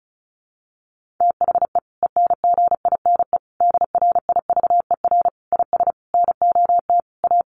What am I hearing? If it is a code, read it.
Morse code, heard as a keyed tone, about 34 wpm, 700 Hz. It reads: THE ENGINE DRIVER IS NOT A
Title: Sentences from Top 1500 Words 35wpm